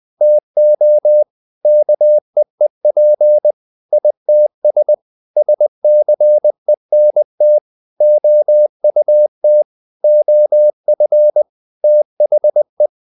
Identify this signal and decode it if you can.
Morse code, heard as a keyed tone, about 20 wpm, 600 Hz. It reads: TO KEEP ITS SCENT OUT OF THE
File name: the old man and the sea part 2 20wpm